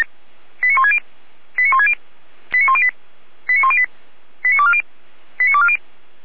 Five-tone selective call (ZVEI), heard as quick runs of five tones, tones of about 70 ms. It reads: 89160 89160 89108 89108 89240 89240